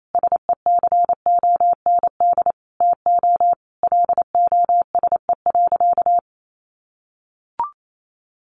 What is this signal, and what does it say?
Morse code, heard as a keyed tone, about 28 wpm, 700 Hz. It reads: SECOND TO LOSE.
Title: Sentences from Top 1100 Words 28wpm